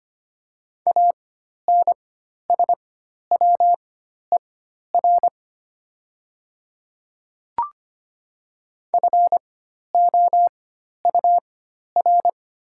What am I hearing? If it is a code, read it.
Morse code, heard as a keyed tone, about 25 wpm, spaced as for 12 wpm, 700 Hz. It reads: ANSWER FOUR
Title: 4x Character Spacing - Top 300 Words - 25wpm